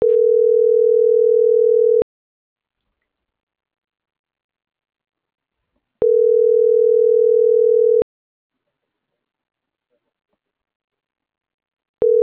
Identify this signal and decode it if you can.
Ringback tone (North America), 440 + 480 Hz.